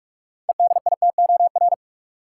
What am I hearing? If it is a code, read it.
Morse code, heard as a keyed tone, about 44 wpm, 700 Hz. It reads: EDITOR
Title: Six Letter Words - 45wpm